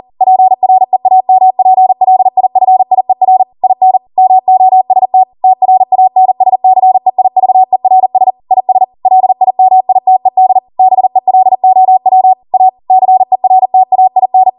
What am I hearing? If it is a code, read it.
Morse code, heard as a keyed tone, about 40 wpm, 750 Hz. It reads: PREAMPLIFIER IN MOST TRANSCEIVERS IS LIMITED BELOW A CERTAIN